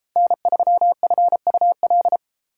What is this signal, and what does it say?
Morse code, heard as a keyed tone, about 29 wpm, 700 Hz. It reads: N3FUL